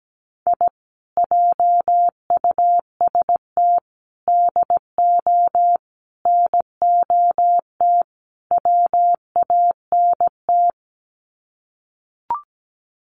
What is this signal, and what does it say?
Morse code, heard as a keyed tone, about 17 wpm, 700 Hz. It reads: I JUST DO NOT WANT
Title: Sets of 5 Words Encore2 17wpm